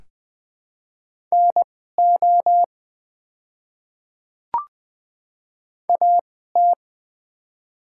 Morse code, heard as a keyed tone, about 20 wpm, spaced as for 14 wpm, 700 Hz. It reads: NO AT
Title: 2x Character Spacing - Top 100 Words - 20wpm